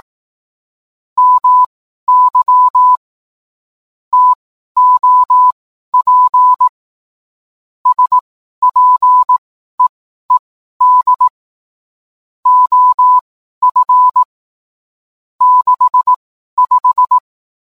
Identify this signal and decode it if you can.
Morse code, heard as a keyed tone, about 18 wpm, spaced as for 12 wpm, 1000 Hz. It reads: MY TOP SPEED OF 65